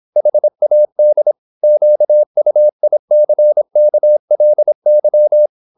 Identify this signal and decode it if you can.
Morse code, heard as a keyed tone, about 26 wpm, 600 Hz. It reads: HAD QUICKLY